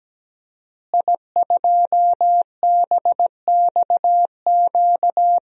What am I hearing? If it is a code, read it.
Morse code, heard as a keyed tone, about 17 wpm, 700 Hz. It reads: I2BXQ
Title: DXCC-100 Call Signs - 5 Characters - Repeated 3x - 17wpm